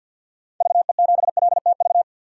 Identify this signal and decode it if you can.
Morse code, heard as a keyed tone, about 47 wpm, 700 Hz. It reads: VE7LTV
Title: DXCC-100 Call Signs - 6 Characters - 50wpm